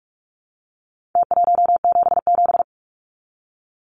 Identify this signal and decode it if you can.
Morse code, heard as a keyed tone, about 44 wpm, 700 Hz. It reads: T177
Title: Numbers - Contest Style - 4-Digit 45wpm